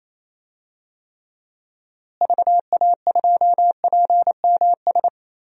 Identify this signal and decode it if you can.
Morse code, heard as a keyed tone, about 28 wpm, 700 Hz. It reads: VA2PMS
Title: DXCC-100 Call Signs - 6 Characters - Repeated 2x - 28wpm